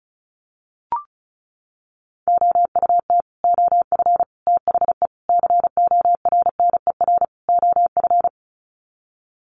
Morse code, heard as a keyed tone, about 34 wpm, 700 Hz. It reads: OUT OF THE CORNER OF
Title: Sets of 5 Words 35wpm